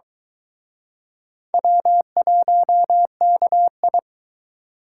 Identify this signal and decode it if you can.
Morse code, heard as a keyed tone, about 23 wpm, 700 Hz. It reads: W1KI